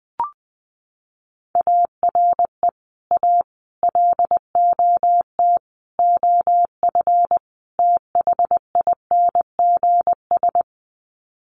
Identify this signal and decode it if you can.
Morse code, heard as a keyed tone, about 20 wpm, 700 Hz. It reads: ARE A LOT OF THINGS
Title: Sets of 5 Words Encore2 20wpm